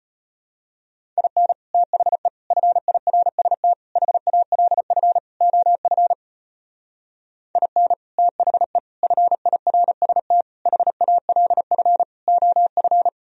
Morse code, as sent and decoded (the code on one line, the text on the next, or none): .. -. / - .... / ..-. .. .-. ... - / .... .- .-.. ..-. / --- ..-. / .. -. / - .... / ..-. .. .-. ... - / .... .- .-.. ..-. / --- ..-.
IN THE FIRST HALF OF IN THE FIRST HALF OF